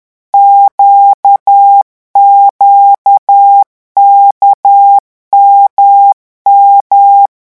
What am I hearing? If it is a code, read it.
Morse code, heard as a keyed tone, about 10 wpm, 800 Hz. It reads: QQKMM
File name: morse mp3 / revision2-vitesse 10 mots